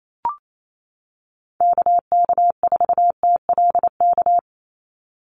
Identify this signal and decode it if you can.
Morse code, heard as a keyed tone, about 28 wpm, 700 Hz. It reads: KK4TLK
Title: USA Call Signs 28wpm